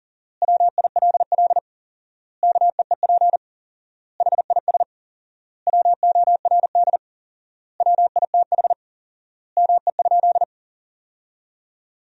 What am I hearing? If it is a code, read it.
Morse code, heard as a keyed tone, about 37 wpm, 700 Hz. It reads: WILL KEEP HIS WORD WITH ME?
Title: 4x Word-Spacing - Sentences from Top 300 Words - 40wpm